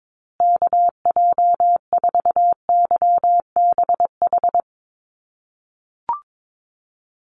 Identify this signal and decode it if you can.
Morse code, heard as a keyed tone, about 22 wpm, 700 Hz. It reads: KJ4YBH
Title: USA Call Signs Repeated 2x 22wpm